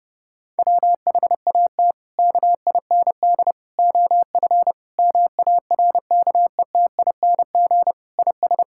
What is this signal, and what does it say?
Morse code, heard as a keyed tone, about 30 wpm, 700 Hz. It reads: WHAT KIND OF MARKETING IS
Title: Sentences from Top 1000 Words - Review Part 5 - 30wpm